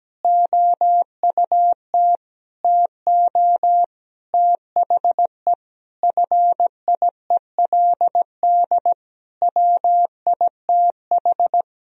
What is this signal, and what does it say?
Morse code, heard as a keyed tone, about 17 wpm, 700 Hz. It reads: OUT TO THE FIELD WITH
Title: Sentences from Top 1000 Words - Review Part 2 - 17wpm